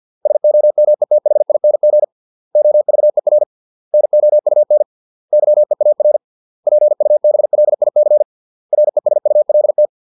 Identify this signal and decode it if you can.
Morse code, heard as a keyed tone, about 50 wpm, 600 Hz. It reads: SOMETHING OVER NOUN YEAR PUBLIC RESULT